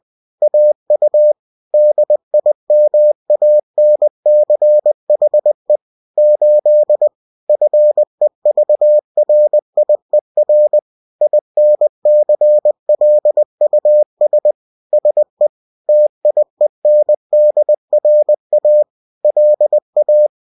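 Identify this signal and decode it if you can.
Morse code, heard as a keyed tone, about 20 wpm, 600 Hz. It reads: AU DIMANCHE 8 FEVRIER INCLUS SE TIENDRA LA
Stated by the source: Bulletin sent in CW.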